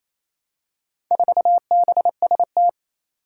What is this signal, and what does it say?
Morse code, heard as a keyed tone, about 27 wpm, 700 Hz. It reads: 4BST
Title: Four Challenging Letter-Number Combinations 28wpm